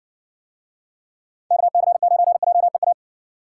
Morse code, heard as a keyed tone, about 60 wpm, 700 Hz. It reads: KC9JEA